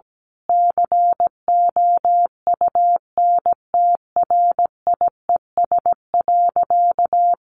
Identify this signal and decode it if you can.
Morse code, heard as a keyed tone, about 17 wpm, 700 Hz. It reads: COUNTRIES.
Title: A Sound Of Thunder 17wpm